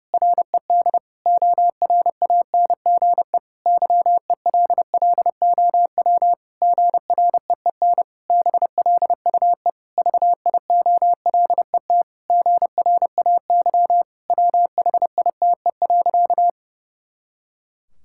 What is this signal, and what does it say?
Morse code, heard as a keyed tone, about 32 wpm, 700 Hz. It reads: RED ORANGE YELLOW GREEN BLUE VIOLET GRAY WHITE.
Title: Pangram Challenge v2 - 30wpm